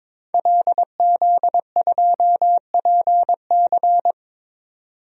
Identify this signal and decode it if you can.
Morse code, heard as a keyed tone, about 23 wpm, 700 Hz. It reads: LZ2PC
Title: DXCC-100 Call Signs - 5 Characters - 22wpm